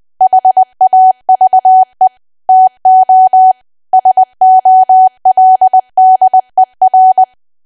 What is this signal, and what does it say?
Morse code, heard as a keyed tone, about 20 wpm, 750 Hz. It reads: HAVE TO SOLDER
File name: cwa / W1AW-Level3 / 20WPM